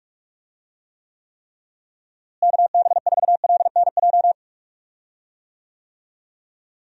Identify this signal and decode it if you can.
Morse code, heard as a keyed tone, about 46 wpm, 700 Hz. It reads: KB4LNJ
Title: USA Call Signs - 6 Characters - Repeated 3x - 45wpm